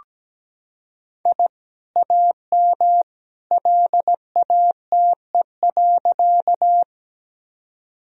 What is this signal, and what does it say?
Morse code, heard as a keyed tone, about 17 wpm, 700 Hz. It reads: I AM LATE.
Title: Sentences from Top 1000 Words - Review Part 2 - 17wpm